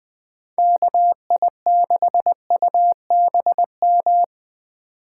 Morse code, heard as a keyed tone, about 20 wpm, 700 Hz. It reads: KI6UBM
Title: Speed Racing - USA 2x3 Call Signs 17wpm